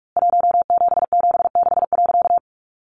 Morse code, heard as a keyed tone, about 44 wpm, 700 Hz. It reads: 1776.
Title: A Sound Of Thunder 45wpm